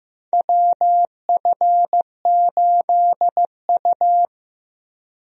Morse code, heard as a keyed tone, about 15 wpm, 700 Hz. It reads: WF8U
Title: USA Call Signs - 4 Characters - Repeated 2x - 15wpm